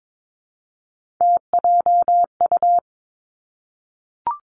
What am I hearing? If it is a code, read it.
Morse code, heard as a keyed tone, about 22 wpm, 700 Hz. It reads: TJU
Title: Three Challenging Letter-Number Combinations 22wpm